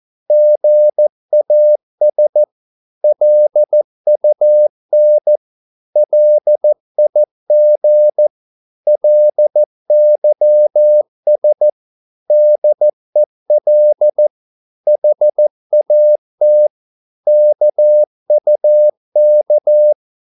Morse code, heard as a keyed tone, about 14 wpm, 600 Hz. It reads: GAS LUN LIG LYS DEL HAT KUK
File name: Korte ord DK 14wpm